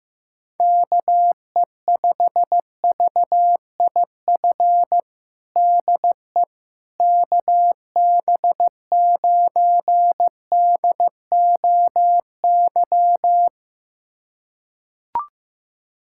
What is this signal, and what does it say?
Morse code, heard as a keyed tone, about 15 wpm, 700 Hz. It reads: KE5VIF DE KB9DOY
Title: USA Callsign DE Callsign - 6 Characters - 15wpm